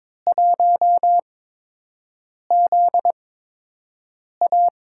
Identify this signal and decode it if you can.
Morse code, heard as a keyed tone, about 22 wpm, spaced as for 6 wpm, 700 Hz. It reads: 1ZA
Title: USA Call Signs - 8x Character Spacing - 22wpm